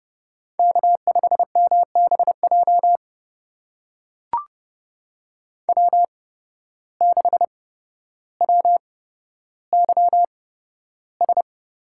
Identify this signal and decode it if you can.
Morse code, heard as a keyed tone, about 29 wpm, spaced as for 13 wpm, 700 Hz. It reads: K5MBJ W6WYS
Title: USA Call Signs - 5 Characters - 8x Character Spacing - 30wpm